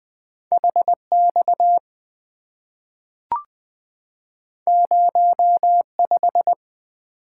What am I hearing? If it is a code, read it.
Morse code, heard as a keyed tone, about 20 wpm, 690 Hz. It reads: HX 05
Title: Two Letter-Number Combinations - 20wpm